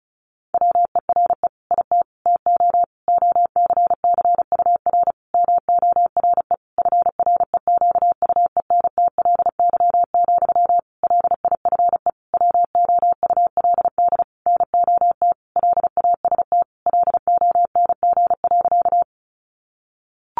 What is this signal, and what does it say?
Morse code, heard as a keyed tone, about 34 wpm, 700 Hz. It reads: WERE IT TO OCCUR MORE FREQUENTLY, LIFE WOULD NOT LAST LONG.
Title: Sentences from Top 1300 Words 35wpm